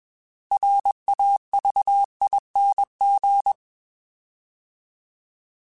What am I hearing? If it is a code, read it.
Morse code, heard as a keyed tone, about 21 wpm, 800 Hz. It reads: RAVING